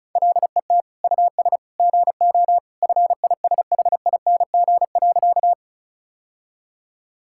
Morse code, heard as a keyed tone, about 34 wpm, 700 Hz. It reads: LET US GO FISHING.
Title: Sentences from Top 1000 Words - Review Part 5 - 35wpm